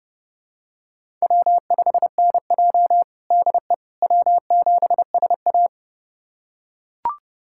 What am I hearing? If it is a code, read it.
Morse code, heard as a keyed tone, about 30 wpm, 700 Hz. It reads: W5NJ DE W7SA
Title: USA Callsign DE Callsign - 4 Characters - 30wpm